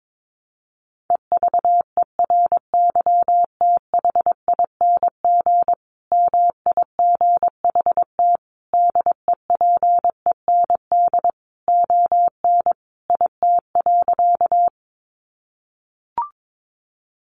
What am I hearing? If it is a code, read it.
Morse code, heard as a keyed tone, about 22 wpm, 700 Hz. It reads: EVERYTHING MIGHT DEPEND ON IT.
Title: Sentences from Top 1300 Words 22wpm